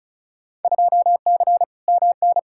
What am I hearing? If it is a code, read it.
Morse code, heard as a keyed tone, about 36 wpm, 700 Hz. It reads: 2C MN
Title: Field Day Exchanges Encore - 35wpm